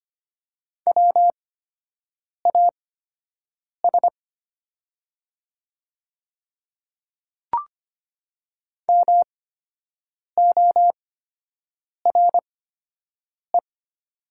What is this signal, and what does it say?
Morse code, heard as a keyed tone, about 25 wpm, spaced as for 7 wpm, 700 Hz. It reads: WAS MORE
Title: 8x Character Spacing - Top 200 Words - 25wpm